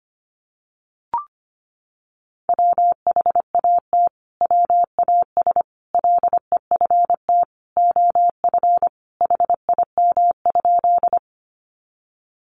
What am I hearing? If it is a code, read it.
Morse code, heard as a keyed tone, about 25 wpm, 700 Hz. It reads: WHAT WAS LEFT OF HIM?
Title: Sentences from Top 300 Words 25wpm